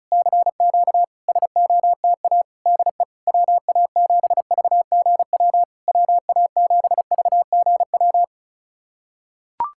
Morse code, heard as a keyed tone, about 36 wpm, 690 Hz. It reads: CQ SOTA DE WA7VGW WA7VGW
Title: SOTA QSO Practice 35wpm